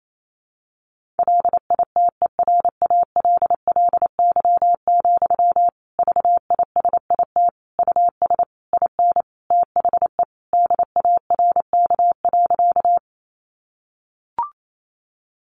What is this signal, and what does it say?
Morse code, heard as a keyed tone, about 28 wpm, 700 Hz. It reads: LITERALLY, VISIT US IN THE DARK.
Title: Sentences from Top 1400 Words 28wpm